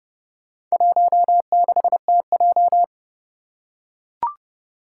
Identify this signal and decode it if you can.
Morse code, heard as a keyed tone, about 29 wpm, 700 Hz. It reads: 16TJ
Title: Four Challenging Letter-Number Combinations 30wpm